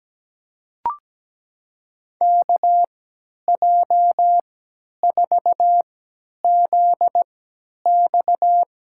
Morse code, heard as a keyed tone, about 17 wpm, spaced as for 10 wpm, 700 Hz. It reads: KJ4ZX
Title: USA Call Signs - 6 Characters - 3x Character Spacing - 17wpm